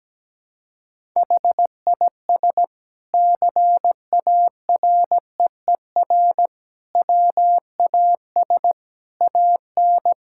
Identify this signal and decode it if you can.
Morse code, heard as a keyed tone, about 17 wpm, 700 Hz. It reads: HIS CAREER WAS AN
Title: Sentences from Top 1000 Words - Review Part 6 - 17wpm